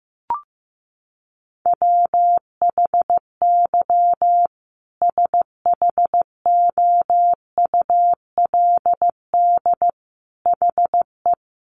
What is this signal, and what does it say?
Morse code, heard as a keyed tone, about 14 wpm, 700 Hz. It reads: WHY SHOULD HE
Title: Sentences from Top 1300 Words 15wpm